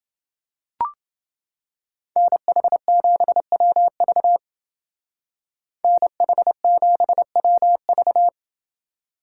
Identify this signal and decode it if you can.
Morse code, heard as a keyed tone, about 28 wpm, 700 Hz. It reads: NH7WV NH7WV